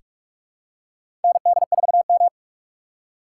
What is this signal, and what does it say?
Morse code, heard as a keyed tone, about 46 wpm, 700 Hz. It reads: ND4M
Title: USA Call Signs - 4 Characters - 4x Character Spacing - 45wpm